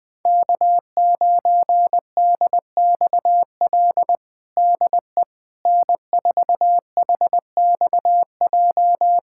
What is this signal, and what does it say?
Morse code, heard as a keyed tone, about 20 wpm, 700 Hz. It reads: K9DXL DE N4HXJ
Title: USA Callsign DE Callsign - 5 Characters - 20wpm